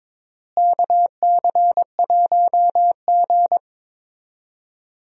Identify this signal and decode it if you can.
Morse code, heard as a keyed tone, about 22 wpm, 700 Hz. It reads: KC1G
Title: USA Call Signs - 4 Characters - Repeated 3x - 22wpm